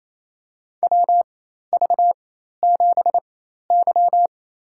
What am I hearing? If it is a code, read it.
Morse code, heard as a keyed tone, about 28 wpm, spaced as for 13 wpm, 700 Hz. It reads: WV7Y
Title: USA Call Signs - 4x Character Spacing - 28wpm